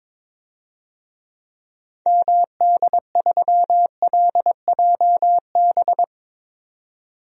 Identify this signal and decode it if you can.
Morse code, heard as a keyed tone, about 21 wpm, 700 Hz. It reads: MD3LJB
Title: DXCC-100 Call Signs - 6 Characters - 22wpm